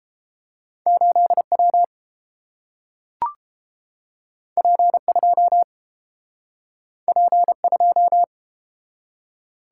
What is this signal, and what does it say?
Morse code, heard as a keyed tone, about 30 wpm, 700 Hz. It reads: 8 W P 2 P 2